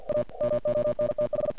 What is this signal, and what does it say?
Morse code, heard as a keyed tone, about 50 wpm, 600 Hz, with noise. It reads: A MONTH
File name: archived / CRSnet